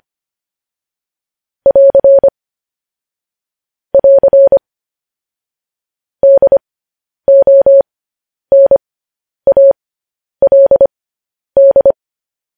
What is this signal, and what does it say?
Morse code, heard as a keyed tone, about 25 wpm, spaced as for 10 wpm, 550 Hz. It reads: <AR> <AR> DONALD
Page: news headlines in morse code at 10 wpm